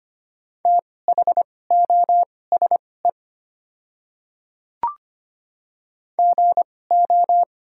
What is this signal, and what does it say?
Morse code, heard as a keyed tone, about 25 wpm, spaced as for 18 wpm, 690 Hz. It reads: THOSE GO
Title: 2x Character Spacing - Top 300 Words - 25wpm